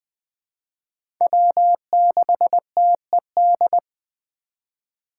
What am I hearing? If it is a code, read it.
Morse code, heard as a keyed tone, about 20 wpm, 700 Hz. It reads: W6TED
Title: 2023 SOTA Chaser Call Signs - 20wpm